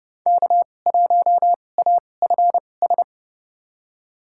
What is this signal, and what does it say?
Morse code, heard as a keyed tone, about 30 wpm, spaced as for 22 wpm, 700 Hz. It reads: K1AFS
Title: USA Call Signs - 5 Characters - 2x Character Spacing - 30wpm